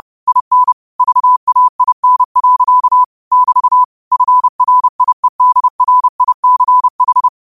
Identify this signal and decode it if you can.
Morse code, heard as a keyed tone, about 30 wpm, 1000 Hz. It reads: IN VAIN. <BT> FRIEDRICH
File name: qotdfiles / 30wpm